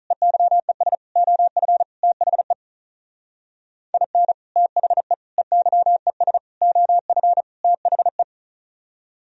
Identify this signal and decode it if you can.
Morse code, heard as a keyed tone, about 45 wpm, 700 Hz. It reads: EYES OF THE IN THE EYES OF THE